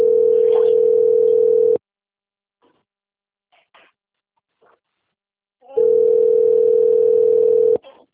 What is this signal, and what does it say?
Ringback tone (North America), 440 + 480 Hz.